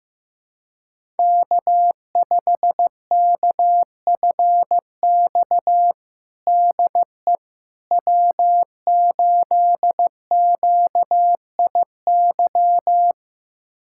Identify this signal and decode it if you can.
Morse code, heard as a keyed tone, about 15 wpm, 700 Hz. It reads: K5KFX DE W8QIY
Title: USA Callsign DE Callsign - 5 Characters - 15wpm